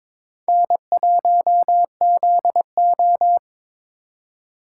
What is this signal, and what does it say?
Morse code, heard as a keyed tone, about 26 wpm, 700 Hz. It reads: N1ZO
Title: Speed Racing - USA 1x2 Call Signs 22wpm